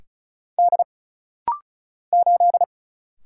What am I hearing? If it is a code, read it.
Morse code, heard as a keyed tone, about 36 wpm, 700 Hz. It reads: D 8
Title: Single Letter-Number - Rapid-Fire 35wpm